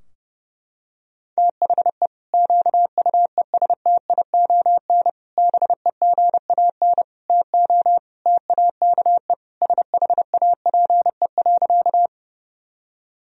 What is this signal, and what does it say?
Morse code, heard as a keyed tone, about 30 wpm, 700 Hz. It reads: THE QUESTION BEGAN TO TAKE SHAPE.
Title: Sentences from Top 1000 Words - Review Part 3 - 30wpm